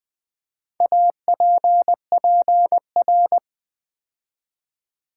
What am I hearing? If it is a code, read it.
Morse code, heard as a keyed tone, about 20 wpm, 700 Hz. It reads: APPR
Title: Build A Word - 4 Letters At A Time - 12-Letter Words - 20wpm